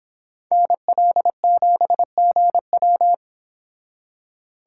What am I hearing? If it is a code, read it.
Morse code, heard as a keyed tone, about 29 wpm, 700 Hz. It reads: NL7GW